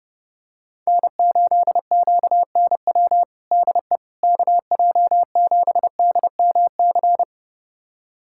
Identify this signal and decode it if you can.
Morse code, heard as a keyed tone, about 30 wpm, 700 Hz. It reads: N8QNW DE KJ7DMC
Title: USA Callsign DE Callsign 30wpm